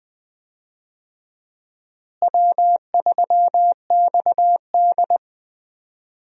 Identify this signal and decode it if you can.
Morse code, heard as a keyed tone, about 21 wpm, 700 Hz. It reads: W3XD
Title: USA Call Signs - 4 Characters - Repeated 3x - 20wpm